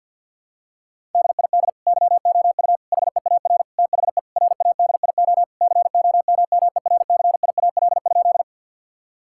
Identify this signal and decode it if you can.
Morse code, heard as a keyed tone, about 52 wpm, 700 Hz. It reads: DID YOU HEAR THE RADIO COMMERCIAL?
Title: Sentences from Top 1000 Words - Review Part 5 - 50wpm